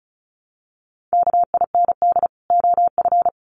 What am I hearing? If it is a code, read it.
Morse code, heard as a keyed tone, about 34 wpm, 700 Hz. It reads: KIND OF
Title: Sets of 2 Words 35wpm